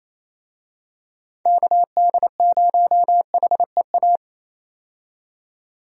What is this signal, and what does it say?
Morse code, heard as a keyed tone, about 27 wpm, 700 Hz. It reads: KD0HEA
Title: USA Call Signs - 6 Characters - 28wpm